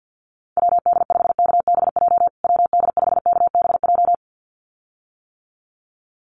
Morse code, heard as a keyed tone, about 50 wpm, 700 Hz. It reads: WD5KBJ WD5KBJ
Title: USA Call Signs Repeated 2x 50wpm